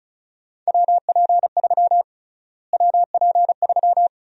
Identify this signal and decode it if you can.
Morse code, heard as a keyed tone, about 36 wpm, 700 Hz. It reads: WP3 WP3
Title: USA Call Sign Prefixes - Repeated 2x - 35wpm